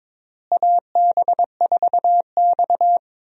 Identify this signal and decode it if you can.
Morse code, heard as a keyed tone, about 21 wpm, 700 Hz. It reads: AB4X
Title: USA Call Signs - 4 Characters - Repeated 3x - 22wpm